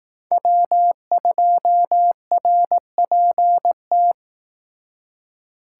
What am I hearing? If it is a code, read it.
Morse code, heard as a keyed tone, about 18 wpm, 700 Hz. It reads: W2RPT
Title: Speed Racing - USA 1x3 Call Signs 15wpm